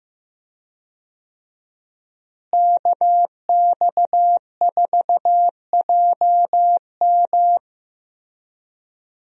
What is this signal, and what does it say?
Morse code, heard as a keyed tone, about 16 wpm, 700 Hz. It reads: KX4JM
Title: USA Call Signs - 5 Characters - Repeated 3x - 15wpm